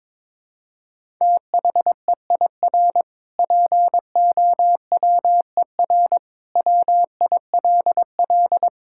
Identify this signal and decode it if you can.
Morse code, heard as a keyed tone, about 22 wpm, 700 Hz. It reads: THEIR POWER WILL
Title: Sentences from Top 1000 Words - Review Part 6 - 22wpm